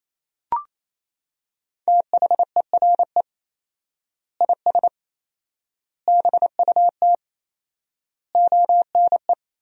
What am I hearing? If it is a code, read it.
Morse code, heard as a keyed tone, about 28 wpm, 700 Hz. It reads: THERE IS BUT ONE
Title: 4x Word-Spacing - Sentences from Top 100 Words - 28wpm